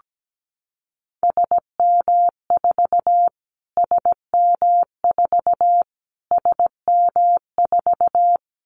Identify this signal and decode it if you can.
Morse code, heard as a keyed tone, about 17 wpm, 700 Hz. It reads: SM4 SM4 SM4
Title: Callsign Prefixes for Top 25 DXCC - Repeated 3x - 17wpm